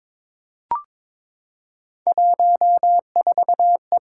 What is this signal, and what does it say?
Morse code, heard as a keyed tone, about 22 wpm, 700 Hz. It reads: 14E
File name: practice / 22wpm / Three Letter-Number Combinations - 22wpm